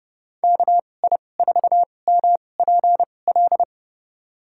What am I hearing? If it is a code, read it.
Morse code, heard as a keyed tone, about 32 wpm, spaced as for 22 wpm, 700 Hz. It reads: KI4MPL
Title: USA Call Signs - 6 Characters - 2x Character Spacing - 30wpm